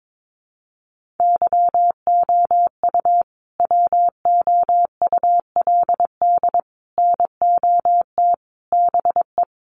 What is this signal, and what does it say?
Morse code, heard as a keyed tone, about 22 wpm, 700 Hz. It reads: YOU WOULD NOT BE
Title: Sentences from Top 1200 Words 22wpm